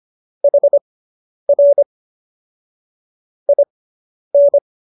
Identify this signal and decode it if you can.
Morse code, heard as a keyed tone, about 25 wpm, spaced as for 10 wpm, 570 Hz. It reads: HR IN